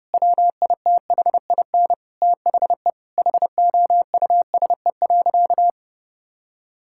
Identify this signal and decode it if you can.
Morse code, heard as a keyed tone, about 29 wpm, 700 Hz. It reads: WITHIN THE HOUSE.
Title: Sentences from Top 1000 Words - Review Part 5 - 30wpm